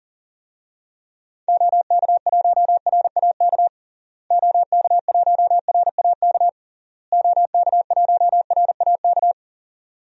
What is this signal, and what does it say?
Morse code, heard as a keyed tone, about 37 wpm, 700 Hz. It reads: OK1RAK OK1RAK OK1RAK
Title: DXCC-100 Call Signs - 6 Characters - Repeated 3x - 40wpm